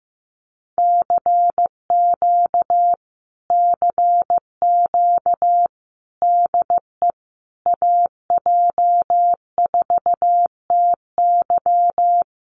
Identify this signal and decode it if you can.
Morse code, heard as a keyed tone, about 14 wpm, 700 Hz. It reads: CQ CQ DE AJ4TY
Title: Common QSO Phrases Encore 15wpm